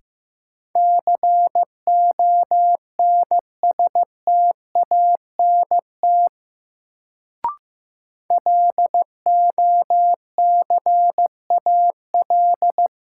Morse code, heard as a keyed tone, about 15 wpm, 700 Hz. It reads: CONSTANT LOCAL
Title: Sentences from Top 1500 Words - New Word Drill Alt - 15wpm